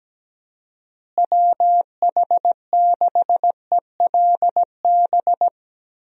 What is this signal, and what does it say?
Morse code, heard as a keyed tone, about 17 wpm, 700 Hz. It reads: WH6ELB
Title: USA Call Signs - 6 Characters - Repeated 3x - 17wpm